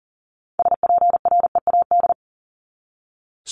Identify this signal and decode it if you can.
Morse code, heard as a keyed tone, about 40 wpm, 700 Hz. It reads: SPREAD
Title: Top 1001-1250 Words 40wpm